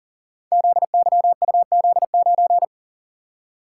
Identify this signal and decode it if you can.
Morse code, heard as a keyed tone, about 40 wpm, 700 Hz. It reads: ZYUZ9
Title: Code Groups - 5 Letter:Numbers - 40wpm